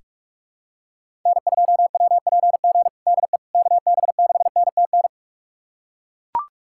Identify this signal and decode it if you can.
Morse code, heard as a keyed tone, about 43 wpm, 700 Hz. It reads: N2WPG DE KB6NTN
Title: USA Callsign DE Callsign 45wpm